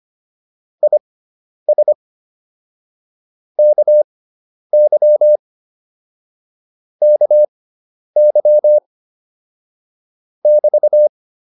Morse code, heard as a keyed tone, about 25 wpm, spaced as for 10 wpm, 600 Hz. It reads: IS KY KY <BT>